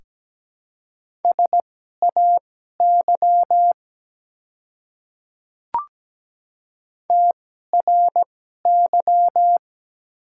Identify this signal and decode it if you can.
Morse code, heard as a keyed tone, about 17 wpm, spaced as for 12 wpm, 700 Hz. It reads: SAY TRY
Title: 2x Character Spacing - Top 200 Words - 17wpm